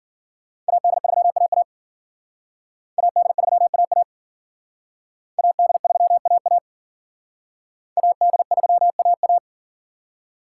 Morse code, heard as a keyed tone, about 52 wpm, 700 Hz. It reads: AD3AA AD3AA AD3AA AD3AA